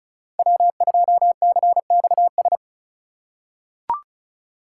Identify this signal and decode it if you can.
Morse code, heard as a keyed tone, about 36 wpm, 700 Hz. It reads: W2CXS
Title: Code Groups - 5 Letter:Numbers - 35wpm